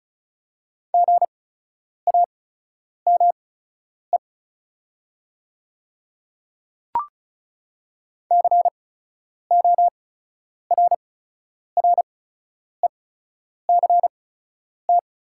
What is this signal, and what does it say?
Morse code, heard as a keyed tone, about 36 wpm, spaced as for 10 wpm, 690 Hz. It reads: GAME CORRECT
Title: 8x Character Spacing - Top 500 Words - 35wpm